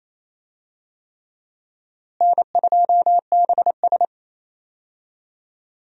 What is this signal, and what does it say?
Morse code, heard as a keyed tone, about 28 wpm, 700 Hz. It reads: N2BS
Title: USA Call Signs - 4 Characters - Repeated 3x - 28wpm